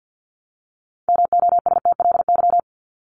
Morse code, heard as a keyed tone, about 49 wpm, 700 Hz. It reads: MOSTLY
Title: LY Suffix - Top 100 Words with Base Words 50wpm